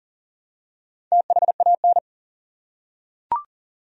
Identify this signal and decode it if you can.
Morse code, heard as a keyed tone, about 40 wpm, 700 Hz. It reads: THAN
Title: Top 100 Words 40wpm